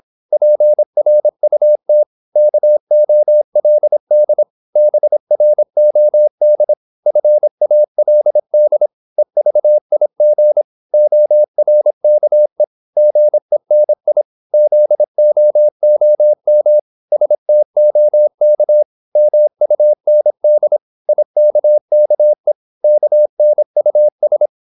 Morse code, heard as a keyed tone, about 26 wpm, 600 Hz. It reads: PRUT KOLD BROD FALD EVIG ORKE GENI ZOOM STOK MUND IKKE KNUS
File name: Korte ord DK 26wpm